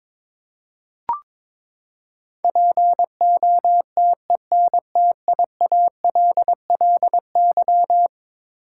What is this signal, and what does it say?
Morse code, heard as a keyed tone, about 22 wpm, 700 Hz. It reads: POTENTIALLY
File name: practice / 22wpm / Eleven Letter Words - 22wpm